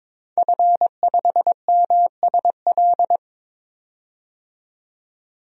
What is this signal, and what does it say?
Morse code, heard as a keyed tone, about 22 wpm, 700 Hz. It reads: F5MSL
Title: DXCC-100 Call Signs - 5 Characters - 22wpm